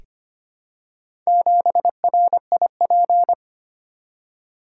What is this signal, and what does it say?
Morse code, heard as a keyed tone, about 25 wpm, 700 Hz. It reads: 7RIP
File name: practice / 25wpm / Four Letter-Number Combinations - 25wpm